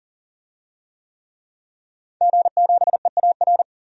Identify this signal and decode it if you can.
Morse code, heard as a keyed tone, about 40 wpm, 700 Hz. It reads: G7EAR
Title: DXCC-100 Call Signs - 5 Characters - 40wpm